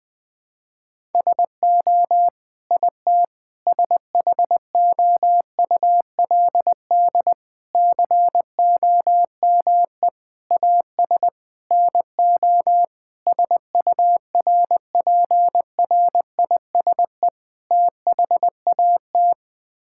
Morse code, heard as a keyed tone, about 21 wpm, 700 Hz. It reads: SO IT SHOULD COME AS NO SURPRISE THAT
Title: Sets of 8 Words - 20wpm